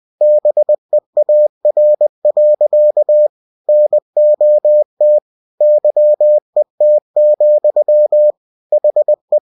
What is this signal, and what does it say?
Morse code, heard as a keyed tone, about 20 wpm, 600 Hz. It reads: BEAR. NOT YET, HE
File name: cw-books / winnie-the-pooh / winnie-20wpm-page-0065